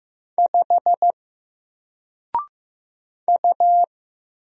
Morse code, heard as a keyed tone, about 15 wpm, 700 Hz. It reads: 5 U
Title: Single Character (Letter:Number:Punctuation:Prosign) 15wpm